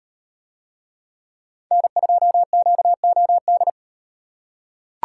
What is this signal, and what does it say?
Morse code, heard as a keyed tone, about 36 wpm, 700 Hz. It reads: N2QOD